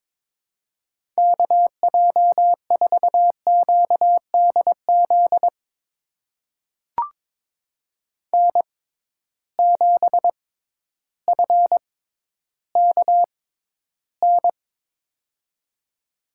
Morse code, heard as a keyed tone, about 22 wpm, spaced as for 12 wpm, 700 Hz. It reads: KJ4QDZ N7FKN
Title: USA Call Signs - 6x Character Spacing - 22wpm